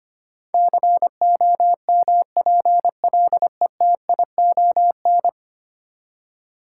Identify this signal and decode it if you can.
Morse code, heard as a keyed tone, about 25 wpm, 700 Hz. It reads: COMPLETION
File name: Ten Letter Words - 25wpm